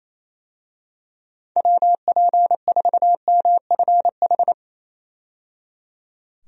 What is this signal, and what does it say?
Morse code, heard as a keyed tone, about 28 wpm, 700 Hz. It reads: WP4MFH
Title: USA Call Signs - 6 Characters - 28wpm